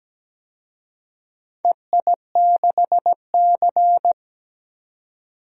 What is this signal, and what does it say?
Morse code, heard as a keyed tone, about 17 wpm, 700 Hz. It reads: EI6C
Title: DXCC-100 Call Signs - 4 Characters - 17wpm